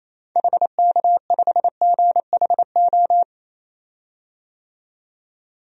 Morse code, heard as a keyed tone, about 28 wpm, 700 Hz. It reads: HK5GHO
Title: DXCC-100 Call Signs - 6 Characters - 28wpm